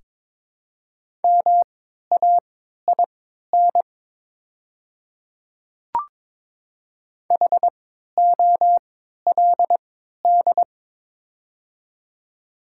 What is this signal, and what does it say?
Morse code, heard as a keyed tone, about 23 wpm, spaced as for 13 wpm, 700 Hz. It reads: MAIN HOLD
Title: 3x Character Spacing - Top 500 Words - 22wpm